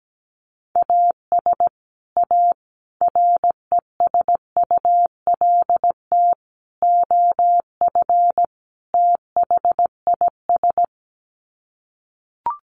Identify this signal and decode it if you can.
Morse code, heard as a keyed tone, about 17 wpm, 700 Hz. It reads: AS A RESULT OF THIS
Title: Sets of 5 Words 17wpm